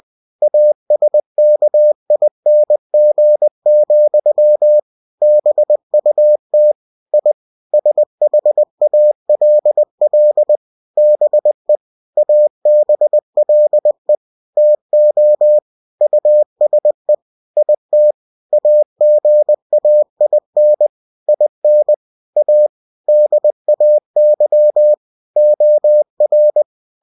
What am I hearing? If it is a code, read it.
Morse code, heard as a keyed tone, about 20 wpm, 600 Hz. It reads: ASKING, BUT I SHALL BE ABLE TO USE IT AGAIN IN A DAY OR
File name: cw-books / winnie-the-pooh / winnie-20wpm-page-0088